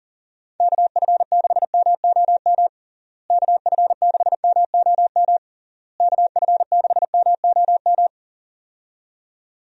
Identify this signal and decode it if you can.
Morse code, heard as a keyed tone, about 40 wpm, 700 Hz. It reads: KF6MOM KF6MOM KF6MOM
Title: USA Call Signs - 6 Characters - Repeated 3x - 40wpm